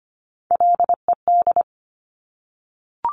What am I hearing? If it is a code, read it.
Morse code, heard as a keyed tone, about 25 wpm, 700 Hz. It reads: LED
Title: Top Three Letter Combinations in Words 25wpm